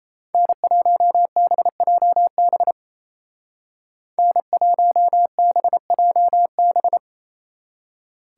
Morse code, heard as a keyed tone, about 35 wpm, 700 Hz. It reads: N1BJB N1BJB
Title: Speed Racing - USA 1x3 Call Signs 28wpm